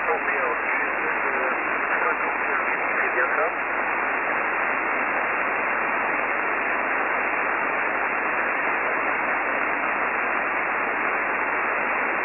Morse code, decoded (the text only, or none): none